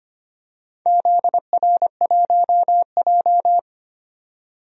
Morse code, heard as a keyed tone, about 25 wpm, 700 Hz. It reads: ZR1J
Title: DXCC-100 Call Signs - 4 Characters - 25wpm